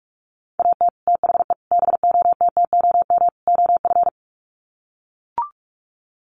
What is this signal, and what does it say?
Morse code, heard as a keyed tone, about 44 wpm, 700 Hz. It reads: AT THE BOTTOM OF
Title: Sets of 4 Words 45wpm